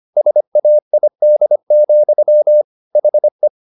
Morse code, heard as a keyed tone, about 25 wpm, 600 Hz. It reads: SAID, HE